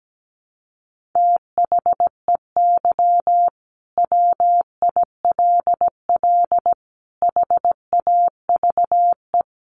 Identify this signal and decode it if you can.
Morse code, heard as a keyed tone, about 17 wpm, 700 Hz. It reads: THEY WILL HAVE
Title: Sentences from Top 800 Words 17wpm